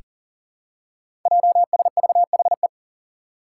Morse code, heard as a keyed tone, about 40 wpm, 700 Hz. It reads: JSVHE
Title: Five Challenging Letter-Number Combinations 40wpm